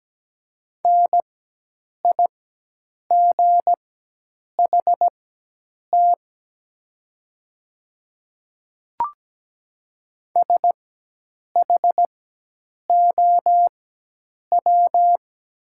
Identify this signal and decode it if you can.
Morse code, heard as a keyed tone, about 17 wpm, spaced as for 8 wpm, 700 Hz. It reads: NIGHT SHOW
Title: 4x Character Spacing - Top 500 Words - 17wpm